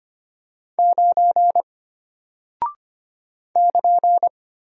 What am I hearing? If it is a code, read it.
Morse code, heard as a keyed tone, about 25 wpm, 700 Hz. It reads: 9 <KN>
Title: Single Character (Letter:Number:Punctuation:Prosign) Alt 25wpm